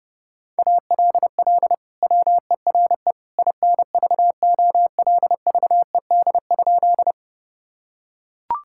Morse code, heard as a keyed tone, about 30 wpm, 700 Hz. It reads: ALL WERE INVOLVED?
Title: Sentences from Top 1000 Words - Review Part 5 - 30wpm